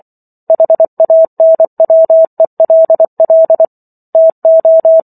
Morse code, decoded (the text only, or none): HANWELL TO